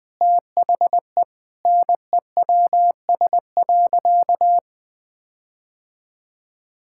Morse code, heard as a keyed tone, about 20 wpm, 700 Hz. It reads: THE NEWS.
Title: Difficult Sentences with Hints - 20wpm